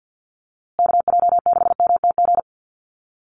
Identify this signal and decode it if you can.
Morse code, heard as a keyed tone, about 50 wpm, 700 Hz. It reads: KJ6MTG
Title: USA Call Signs Repeated 3x 50wpm